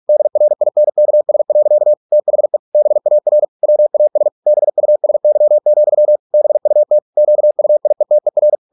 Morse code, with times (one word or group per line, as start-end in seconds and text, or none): 0.1-1.9 DRINKS.
2.1-2.6 THE
2.8-3.4 BAR
3.6-4.3 WAS
4.5-6.2 BUSY,
6.3-7.0 BUT
7.2-8.6 QUIETER